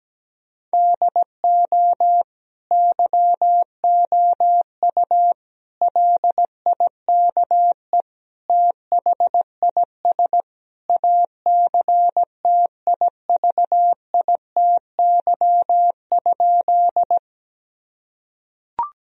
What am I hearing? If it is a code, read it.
Morse code, heard as a keyed tone, about 17 wpm, 700 Hz. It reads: DO YOU LIKE THIS ACTIVITY?
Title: Sentences from Top 1000 Words - Review Part 4 - 17wpm